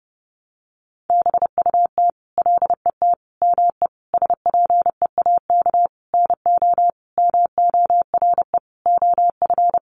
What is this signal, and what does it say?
Morse code, heard as a keyed tone, about 30 wpm, 700 Hz. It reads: BUT LET ME SPEAK NO MORE OF
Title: Sentences from Top 1100 Words 30wpm